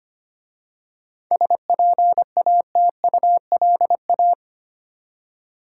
Morse code, heard as a keyed tone, about 25 wpm, 700 Hz. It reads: SPATULA
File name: practice / 25wpm / Easy Words with Hints - 25wpm